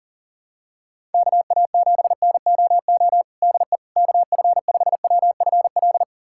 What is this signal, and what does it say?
Morse code, heard as a keyed tone, about 40 wpm, 700 Hz. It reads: KA7NOO DE KF5WFL
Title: USA Callsign DE Callsign 40wpm